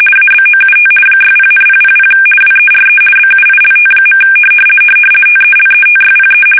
RTTY, about 50 baud. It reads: ABOUT 60,000 WERE BUILT. TELETYPE
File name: fsk850-50B / teletype-corp